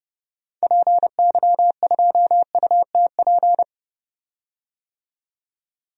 Morse code, heard as a keyed tone, about 29 wpm, 690 Hz. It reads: PY2UTP
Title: DXCC-100 Call Signs - 6 Characters - 30wpm